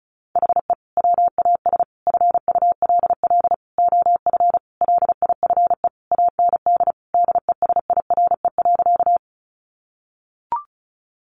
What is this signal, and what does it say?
Morse code, heard as a keyed tone, about 34 wpm, 700 Hz. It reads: HE WAS FULL OF LIFE AND DESIRE.
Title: Sentences from Top 1100 Words 35wpm